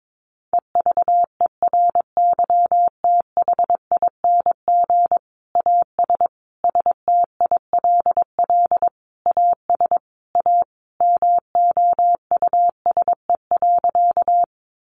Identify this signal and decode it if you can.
Morse code, heard as a keyed tone, about 22 wpm, 700 Hz. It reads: EVERYTHING AS STILL AS A MOUSE.
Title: Sentences from Top 700 Words 22wpm